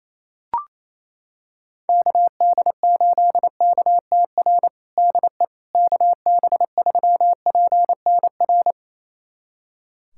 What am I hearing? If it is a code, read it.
Morse code, heard as a keyed tone, about 28 wpm, 700 Hz. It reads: KD8KTR DE KB3PNR
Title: USA Callsign DE Callsign - 6 Characters - 28wpm